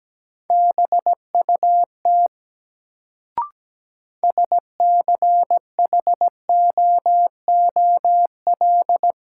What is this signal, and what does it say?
Morse code, heard as a keyed tone, about 17 wpm, 700 Hz. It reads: BUT SCHOOL
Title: Sets of 2 Words - New Word Drill - 17wpm